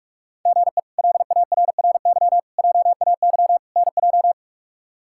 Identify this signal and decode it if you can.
Morse code, heard as a keyed tone, about 42 wpm, 700 Hz. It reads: GE LARRY JAY NJ
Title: K1USN SST QSO Practice - 45wpm